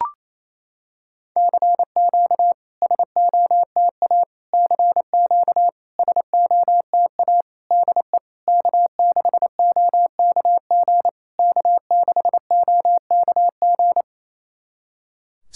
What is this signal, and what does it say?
Morse code, heard as a keyed tone, about 28 wpm, 690 Hz. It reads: CQ SOTA CQ SOTA DE K6OKG K6OKG
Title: SOTA QSO Practice 28wpm